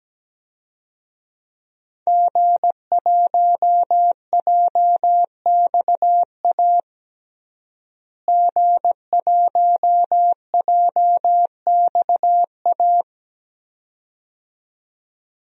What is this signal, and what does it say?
Morse code, heard as a keyed tone, about 17 wpm, 700 Hz. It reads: G1JXA G1JXA
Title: DXCC-100 Call Signs - 5 Characters - Repeated 2x - 17wpm